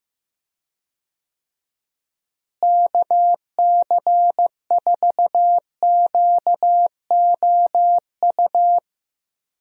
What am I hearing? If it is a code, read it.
Morse code, heard as a keyed tone, about 16 wpm, 700 Hz. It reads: KC4QOU
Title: USA Call Signs - 6 Characters - Repeated 2x - 15wpm